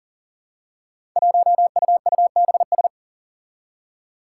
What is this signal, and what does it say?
Morse code, heard as a keyed tone, about 40 wpm, 700 Hz. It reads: 1UUBS
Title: Five Challenging Letter-Number Combinations 40wpm